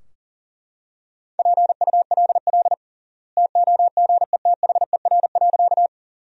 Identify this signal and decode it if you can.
Morse code, heard as a keyed tone, about 40 wpm, 700 Hz. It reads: PULL TOGETHER.
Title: 3x Word-Spacing - Sentences from Top 400 Words - 40wpm